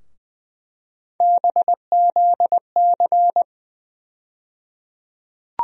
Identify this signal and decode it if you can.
Morse code, heard as a keyed tone, about 21 wpm, 700 Hz. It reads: BZC
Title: USA Call Sign Postfixes - 3 Letters 20wpm